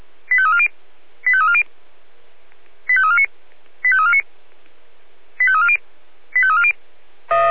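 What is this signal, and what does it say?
Five-tone selective call (ZVEI), heard as quick runs of five tones, tones of about 70 ms. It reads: 85340 85340 85349 85349 85340 85340
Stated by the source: ZVEI five-tone selective call 85274 received by SDR.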